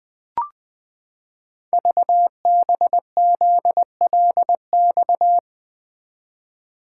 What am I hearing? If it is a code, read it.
Morse code, heard as a keyed tone, about 20 wpm, 700 Hz. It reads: VBZLX
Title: Code Groups - 5 Letters - 20wpm